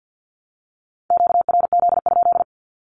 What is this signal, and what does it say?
Morse code, heard as a keyed tone, about 50 wpm, 700 Hz. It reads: QRZ?
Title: Q-Codes 50wpm